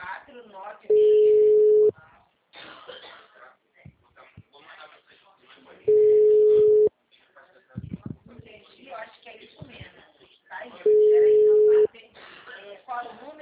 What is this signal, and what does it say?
Ringback tone (Europe, 425 Hz), with speech.